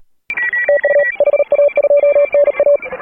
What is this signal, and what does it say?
Morse code, heard as a keyed tone, about 42 wpm, 590 Hz. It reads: TU HA2NA
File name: AUDIO2018 / HA3FMR CW